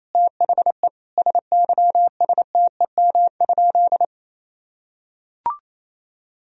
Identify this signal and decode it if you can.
Morse code, heard as a keyed tone, about 28 wpm, 700 Hz. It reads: THE SYSTEM?
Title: Sentences from Top 1000 Words - Review Part 3 - 28wpm